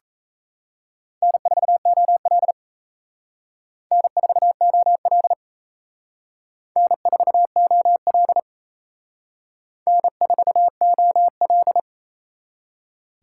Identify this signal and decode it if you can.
Morse code, heard as a keyed tone, about 35 wpm, 700 Hz. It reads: N4OL N4OL N4OL N4OL
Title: Speed Racing - USA 1x2 Call Signs 28wpm